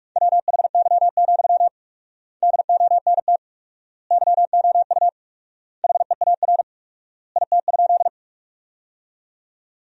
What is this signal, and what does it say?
Morse code, heard as a keyed tone, about 46 wpm, 700 Hz. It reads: WHY, DONT YOU HEAR IT?
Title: 4x Word-Spacing - Sentences from Top 300 Words - 45wpm